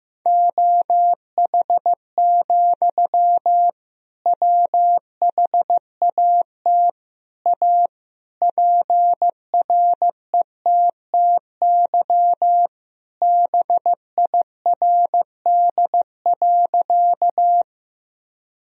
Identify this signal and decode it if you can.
Morse code, heard as a keyed tone, about 15 wpm, 700 Hz. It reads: OH, WHAT A PRETTY BIRD.
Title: Sentences from Top 1000 Words - Review Part 2 - 15wpm